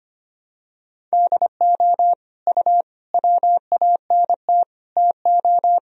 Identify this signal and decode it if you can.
Morse code, heard as a keyed tone, about 25 wpm, 700 Hz. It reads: DO U WANT TO
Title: Sentences from Top 300 Words and Abbreviations 25wpm